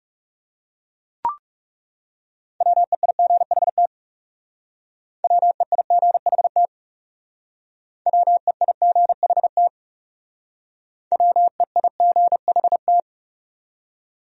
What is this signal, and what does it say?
Morse code, heard as a keyed tone, about 36 wpm, 700 Hz. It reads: WEIGHT WEIGHT WEIGHT WEIGHT
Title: Speed-Racing Top 500 Words 30wpm